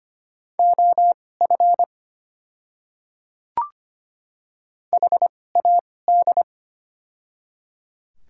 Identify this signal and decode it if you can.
Morse code, heard as a keyed tone, about 25 wpm, spaced as for 18 wpm, 700 Hz. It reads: OF HAD
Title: 2x Character Spacing - Top 100 Words - 25wpm